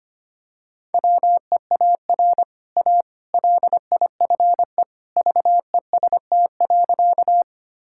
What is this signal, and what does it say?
Morse code, heard as a keyed tone, about 25 wpm, 700 Hz. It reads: WEAR A LIFE VEST.
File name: Easy Sentences with Hints - 25wpm